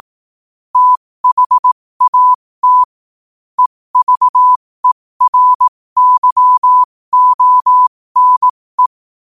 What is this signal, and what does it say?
Morse code, heard as a keyed tone, about 18 wpm, spaced as for 15 wpm, 1000 Hz. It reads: THAT EVERYONE
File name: qotdfiles / 15wpm